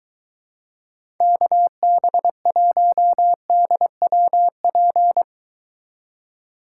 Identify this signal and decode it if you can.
Morse code, heard as a keyed tone, about 23 wpm, 700 Hz. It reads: KB1DWP